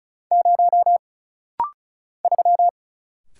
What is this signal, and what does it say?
Morse code, heard as a keyed tone, about 36 wpm, 700 Hz. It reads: 0 3
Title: Single Letter-Number - Rapid-Fire 35wpm